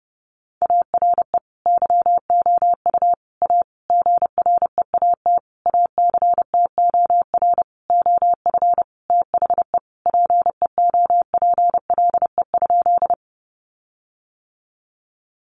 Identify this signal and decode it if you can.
Morse code, heard as a keyed tone, about 30 wpm, 700 Hz. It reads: ARE YOU A GREAT ACTOR OF THE PEOPLE?
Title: Sentences from Top 1500 Words 30wpm